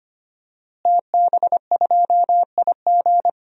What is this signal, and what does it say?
Morse code, heard as a keyed tone, about 25 wpm, 700 Hz. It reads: TB2IG
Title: DXCC-100 Call Signs - 5 Characters - Repeated 3x - 25wpm